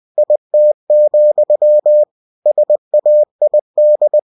Morse code, heard as a keyed tone, about 20 wpm, 600 Hz. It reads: IT, SAID
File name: winnie-20wpm